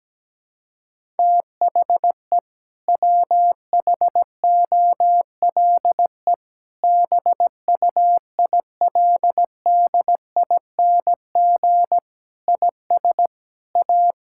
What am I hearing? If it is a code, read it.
Morse code, heard as a keyed tone, about 17 wpm, 700 Hz. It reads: THE WHOLE BUILDING IS A
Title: Sentences from Top 1000 Words - Review Part 5 - 17wpm